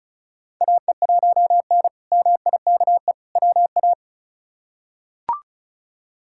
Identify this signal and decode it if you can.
Morse code, heard as a keyed tone, about 36 wpm, 700 Hz. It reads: AE1N MIKE WA
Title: K1USN SST QSO Practice - 35wpm